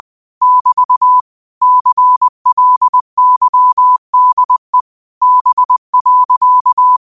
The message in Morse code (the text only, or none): <BT> CLYDE B.